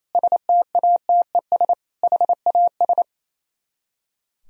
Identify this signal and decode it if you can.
Morse code, heard as a keyed tone, about 32 wpm, 700 Hz. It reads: STATES HAS
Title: Speed-Racing - Sets of 4 Words 28wpm